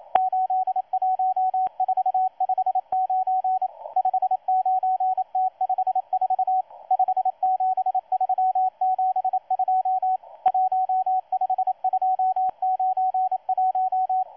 Morse code, heard as a keyed tone, about 27 wpm, 750 Hz, with radio noise. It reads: 81459 59T54 57372 15291
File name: M12-2016-07-07-1310utc-13926khz-error-restart